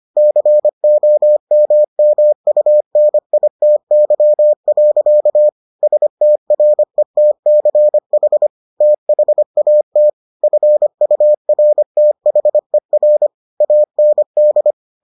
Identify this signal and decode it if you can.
Morse code, heard as a keyed tone, about 25 wpm, 600 Hz. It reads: COMMUNITY. STRETCH THAT FURTHER AND